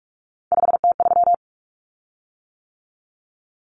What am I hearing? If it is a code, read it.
Morse code, heard as a keyed tone, about 44 wpm, 700 Hz. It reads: 5T3
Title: Common Cut Numbers - 3-Digit 45wpm